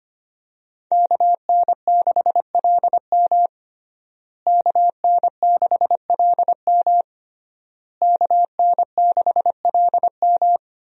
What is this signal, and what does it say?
Morse code, heard as a keyed tone, about 25 wpm, 700 Hz. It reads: KN6LM KN6LM KN6LM
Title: USA Call Signs - 5 Characters - Repeated 3x - 25wpm